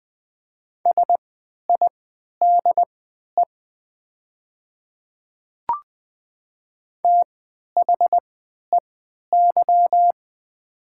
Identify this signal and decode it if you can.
Morse code, heard as a keyed tone, about 20 wpm, spaced as for 11 wpm, 700 Hz. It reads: SIDE THEY
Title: 3x Character Spacing - Top 100 Words - 20wpm